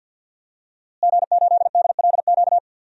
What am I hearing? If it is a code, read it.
Morse code, heard as a keyed tone, about 52 wpm, 700 Hz. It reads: G8DLQ
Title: DXCC-100 Call Signs - 5 Characters - Repeated 2x - 50wpm